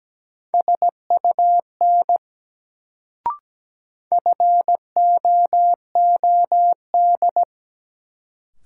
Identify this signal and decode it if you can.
Morse code, heard as a keyed tone, about 17 wpm, 700 Hz. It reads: SUN FOOD
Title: Sentences from Top 300 Words - New Word Drill - 17wpm